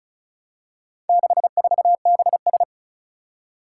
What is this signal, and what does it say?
Morse code, heard as a keyed tone, about 36 wpm, 700 Hz. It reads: 64BS
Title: Four Challenging Letter-Number Combinations 35wpm